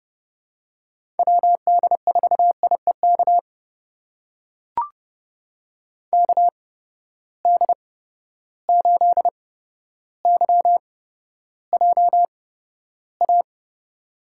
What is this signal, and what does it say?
Morse code, heard as a keyed tone, about 30 wpm, spaced as for 13 wpm, 700 Hz. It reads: WD4IEK KD8YJA
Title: USA Call Signs - 6 Characters - 8x Character Spacing - 30wpm